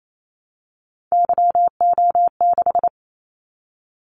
Morse code, heard as a keyed tone, about 28 wpm, 700 Hz. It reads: YO6
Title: Callsign Prefixes for Top 25 DXCC - 28wpm